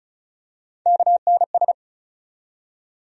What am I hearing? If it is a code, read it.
Morse code, heard as a keyed tone, about 34 wpm, 700 Hz. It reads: KNS
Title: USA Call Sign Postfixes - 3 Letters 35wpm